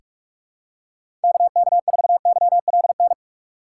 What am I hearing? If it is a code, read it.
Morse code, heard as a keyed tone, about 46 wpm, 700 Hz. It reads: KK4YLN
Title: USA Call Signs - 6 Characters - 3x Character Spacing - 45wpm